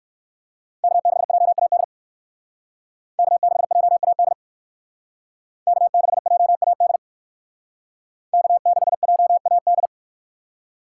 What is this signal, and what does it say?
Morse code, heard as a keyed tone, about 53 wpm, 700 Hz. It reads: K6JAD K6JAD K6JAD K6JAD
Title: Speed Racing - USA 1x3 Call Signs 45wpm